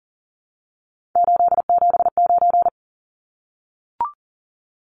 Morse code, heard as a keyed tone, about 40 wpm, 700 Hz. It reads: 879
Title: Numbers - 3-Digit 40wpm